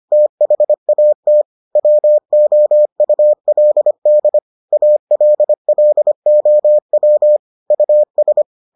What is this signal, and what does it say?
Morse code, heard as a keyed tone, about 25 wpm, 600 Hz. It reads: THAT WOULD ALLOW US